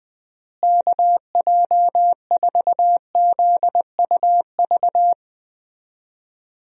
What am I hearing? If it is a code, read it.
Morse code, heard as a keyed tone, about 20 wpm, 700 Hz. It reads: KJ4ZUV
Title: USA Call Signs - 6 Characters - Repeated 2x - 20wpm